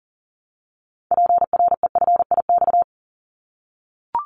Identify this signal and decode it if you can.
Morse code, heard as a keyed tone, about 39 wpm, 700 Hz. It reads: PREFIX
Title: Callsign Prefixes for Top 100 DXCC - 40wpm